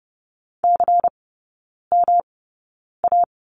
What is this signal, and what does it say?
Morse code, heard as a keyed tone, about 30 wpm, spaced as for 9 wpm, 700 Hz. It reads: CMA
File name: Lesson 19e - Letter w - Three Characters 30wpm